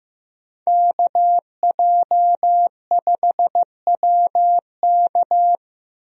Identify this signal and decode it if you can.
Morse code, heard as a keyed tone, about 15 wpm, 700 Hz. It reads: KJ5WK
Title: USA Call Signs - 5 Characters - Repeated 2x - 15wpm